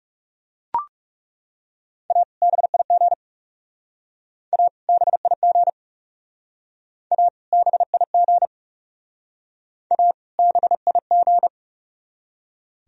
Morse code, heard as a keyed tone, about 36 wpm, 700 Hz. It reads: A BIG A BIG A BIG A BIG
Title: Speed-Racing - Sets of 2 Words 30wpm